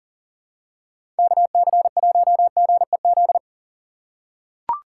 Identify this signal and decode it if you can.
Morse code, heard as a keyed tone, about 40 wpm, 700 Hz. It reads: KC1GEZ
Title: USA Call Signs - 6 Characters - 2x Character Spacing - 40wpm